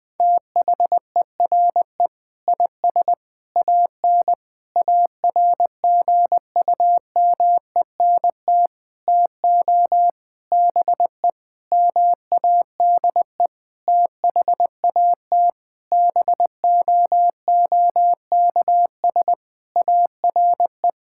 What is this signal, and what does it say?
Morse code, heard as a keyed tone, about 20 wpm, 700 Hz. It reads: THERE IS AN ARGUMENT TO BE MADE THAT BOOKS ARE
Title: Sentences with 8-Word Phrases - Spoken Beginning - 20wpm